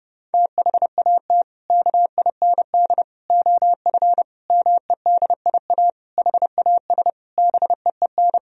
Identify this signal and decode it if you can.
Morse code, heard as a keyed tone, about 30 wpm, 700 Hz. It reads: THAT KIND OF MEDIA HAS BEEN
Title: Sentences from Top 1000 Words - Review Part 4 - 30wpm